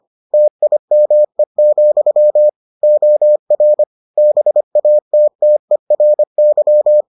Morse code, heard as a keyed tone, about 25 wpm, 600 Hz. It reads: TIME, OR BATTERY